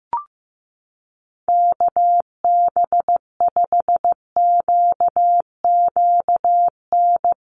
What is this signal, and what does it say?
Morse code, heard as a keyed tone, about 15 wpm, 700 Hz. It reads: KB5QQN
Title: USA Call Signs Repeated 3x 15wpm